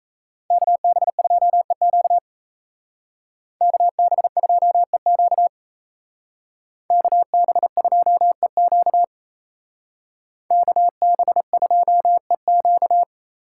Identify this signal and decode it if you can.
Morse code, heard as a keyed tone, about 34 wpm, 700 Hz. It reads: KB2EQ KB2EQ KB2EQ KB2EQ
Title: Speed Racing - USA 2x2 Call Signs 28wpm